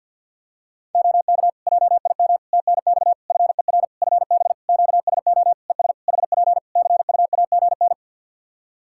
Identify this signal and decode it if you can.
Morse code, heard as a keyed tone, about 50 wpm, 700 Hz. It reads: OK JIM TNX FER FB QSO ES HP CUAGN
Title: Practice QSO - Exchange 4 - Ending - 50wpm